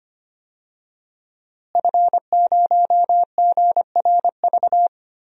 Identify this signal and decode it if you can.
Morse code, heard as a keyed tone, about 25 wpm, 700 Hz. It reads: F0GRV
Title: DXCC-100 Call Signs - 5 Characters - Repeated 2x - 25wpm